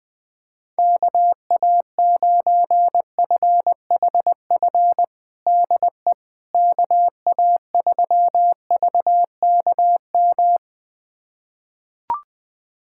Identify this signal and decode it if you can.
Morse code, heard as a keyed tone, about 20 wpm, 700 Hz. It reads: KA9FHF DE KA3VKM
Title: USA Callsign DE Callsign - 6 Characters - 20wpm